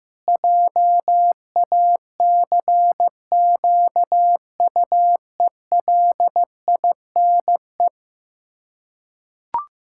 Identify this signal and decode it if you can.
Morse code, heard as a keyed tone, about 15 wpm, 700 Hz. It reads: JACQUELINE
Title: Practice QSO - Exchange 1 - Beginning - 15wpm